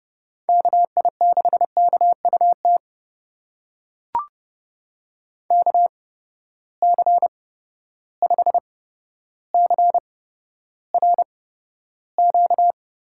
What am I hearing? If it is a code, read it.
Morse code, heard as a keyed tone, about 30 wpm, spaced as for 13 wpm, 700 Hz. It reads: KI6KUT KC5CRQ
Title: USA Call Signs - 6 Characters - 8x Character Spacing - 30wpm